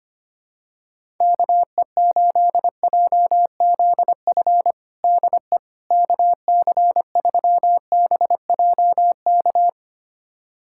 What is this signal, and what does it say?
Morse code, heard as a keyed tone, about 25 wpm, 700 Hz. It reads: KE8JZF DE KC3BJK
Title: USA Callsign DE Callsign - 6 Characters - 25wpm